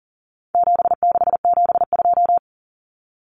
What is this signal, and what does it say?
Morse code, heard as a keyed tone, about 40 wpm, 700 Hz. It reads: 7672
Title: Common Cut Numbers - 4-Digit 40wpm